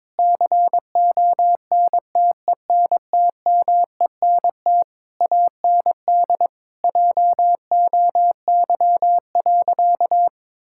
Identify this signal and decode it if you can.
Morse code, heard as a keyed tone, about 22 wpm, 700 Hz. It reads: CONTENTMENT AND JOY.
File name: Difficult Sentences with Hints - 22wpm